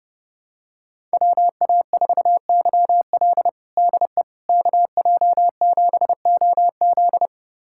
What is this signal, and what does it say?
Morse code, heard as a keyed tone, about 30 wpm, 700 Hz. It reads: WA4YL DE KJ7OZ
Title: USA Callsign DE Callsign 30wpm